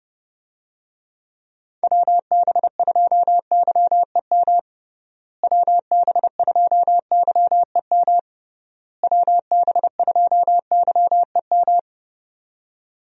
Morse code, heard as a keyed tone, about 30 wpm, 700 Hz. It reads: WB2YEM WB2YEM WB2YEM
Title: USA Call Signs - 6 Characters - Repeated 3x - 30wpm